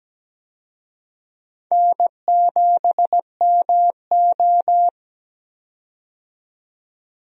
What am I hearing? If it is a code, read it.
Morse code, heard as a keyed tone, about 17 wpm, 700 Hz. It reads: N7MO
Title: USA Call Signs - 4 Characters - 17wpm